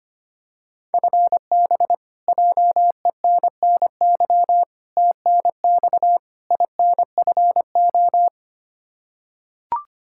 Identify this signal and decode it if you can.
Morse code, heard as a keyed tone, about 25 wpm, 700 Hz. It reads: FB JENNY TNX INFO
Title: Practice QSO - Exchange 3 - More Info - 25wpm